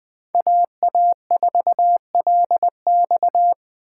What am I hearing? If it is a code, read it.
Morse code, heard as a keyed tone, about 21 wpm, 700 Hz. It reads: AA4LX
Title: Speed Racing - USA 2x2 Call Signs 17wpm